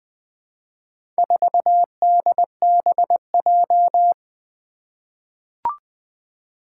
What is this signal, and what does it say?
Morse code, heard as a keyed tone, about 20 wpm, 700 Hz. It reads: 4DBJ
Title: Four Challenging Letter-Number Combinations 20wpm